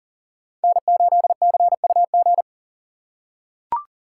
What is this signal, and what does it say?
Morse code, heard as a keyed tone, about 40 wpm, 700 Hz. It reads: N8CUG
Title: USA Call Signs - 5 Characters - 4x Character Spacing - 40wpm